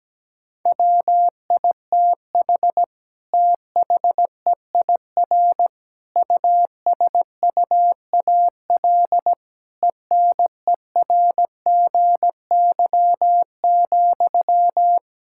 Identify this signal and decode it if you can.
Morse code, heard as a keyed tone, about 17 wpm, 700 Hz. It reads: WITH THEIR USUAL ENERGY,
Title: Sentences from Top 1000 Words - Review Part 6 - 17wpm